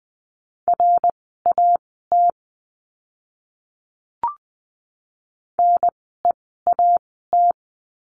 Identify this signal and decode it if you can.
Morse code, heard as a keyed tone, about 20 wpm, spaced as for 14 wpm, 700 Hz. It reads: RAT NEAT
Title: 054 Lesson 10f - Letter r - Words 20wpm